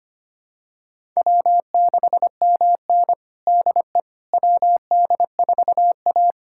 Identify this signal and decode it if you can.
Morse code, heard as a keyed tone, about 25 wpm, 700 Hz. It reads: W6MN DE WD4A
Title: USA Callsign DE Callsign - 4 Characters - 25wpm